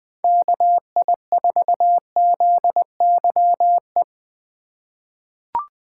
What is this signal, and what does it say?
Morse code, heard as a keyed tone, about 21 wpm, 700 Hz. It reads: KI4ZYE
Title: USA Call Signs - 6 Characters - 2x Character Spacing - 20wpm